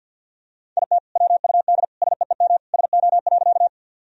Morse code, heard as a keyed tone, about 52 wpm, 700 Hz. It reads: IT WUD SEEM SO.
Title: Sentences from Top 300 Words and Abbreviations 50wpm